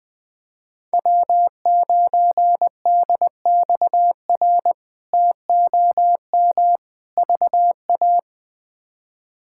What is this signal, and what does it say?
Morse code, heard as a keyed tone, about 20 wpm, 700 Hz. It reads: W9DXR TOM VA
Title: K1USN SST QSO Practice - 20wpm